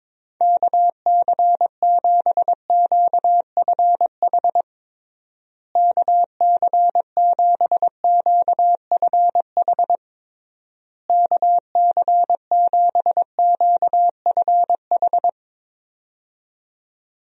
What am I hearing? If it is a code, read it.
Morse code, heard as a keyed tone, about 22 wpm, 700 Hz. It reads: KC7QFH KC7QFH KC7QFH
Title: USA Call Signs - 6 Characters - Repeated 3x - 22wpm